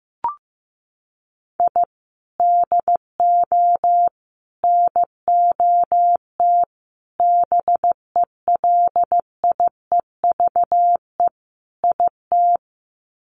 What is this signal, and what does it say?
Morse code, heard as a keyed tone, about 15 wpm, 700 Hz. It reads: I DO NOT BELIEVE IT
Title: Sets of 5 Words 15wpm